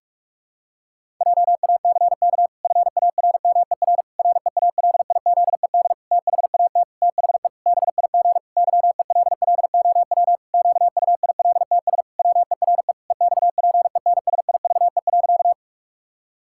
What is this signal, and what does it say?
Morse code, heard as a keyed tone, about 46 wpm, 700 Hz. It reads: JACK FARMER REALIZED THAT THE BIG YELLOW QUILTS WERE EXPENSIVE.
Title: Pangram Challenge v2 - 45wpm